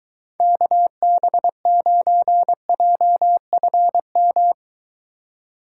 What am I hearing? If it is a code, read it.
Morse code, heard as a keyed tone, about 23 wpm, 700 Hz. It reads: KB9JFM